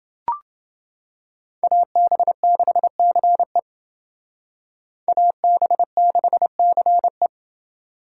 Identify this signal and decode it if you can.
Morse code, heard as a keyed tone, about 29 wpm, 700 Hz. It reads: AB6CE AB6CE